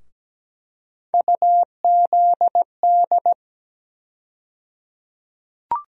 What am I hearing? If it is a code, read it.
Morse code, heard as a keyed tone, about 17 wpm, 700 Hz. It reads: UZD
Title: USA Call Sign Postfixes - 3 Letters 17wpm